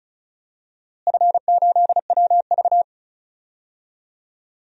Morse code, heard as a keyed tone, about 36 wpm, 700 Hz. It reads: F8WV
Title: DXCC-100 Call Signs - 4 Characters - Repeated 3x - 35wpm